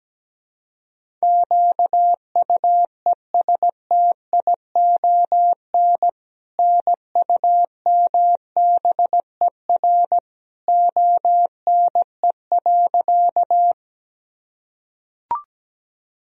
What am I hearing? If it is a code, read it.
Morse code, heard as a keyed tone, about 17 wpm, 700 Hz. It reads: QUESTION NUMBER ONE.
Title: Sentences from Top 1000 Words - Review Part 1 - 17wpm